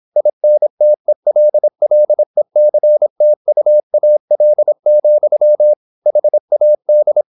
Morse code, heard as a keyed tone, about 26 wpm, 600 Hz. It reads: INTELLECTUAL, HAD
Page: Never 26wpm | CW med Gnister